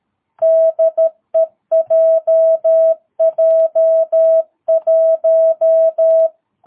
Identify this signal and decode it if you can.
Morse code, heard as a keyed tone, about 13 wpm, 650 Hz. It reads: DE JJ1
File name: jh8cbh / 2022cw